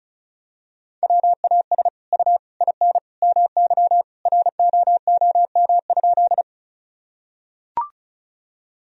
Morse code, heard as a keyed tone, about 34 wpm, 700 Hz. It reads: WAS U IN MY ROOM?
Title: Sentences from Top 300 Words and Abbreviations 35wpm